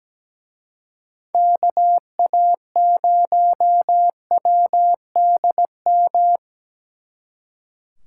Morse code, heard as a keyed tone, about 20 wpm, 700 Hz. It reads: KA0WDM
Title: Speed Racing - USA 2x3 Call Signs 17wpm